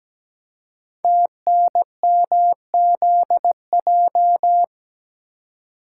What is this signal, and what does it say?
Morse code, heard as a keyed tone, about 17 wpm, 700 Hz. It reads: TNMZJ
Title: Code Groups - 5 Letters - 17wpm